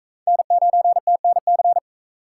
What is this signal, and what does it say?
Morse code, heard as a keyed tone, about 37 wpm, 700 Hz. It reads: N9TNC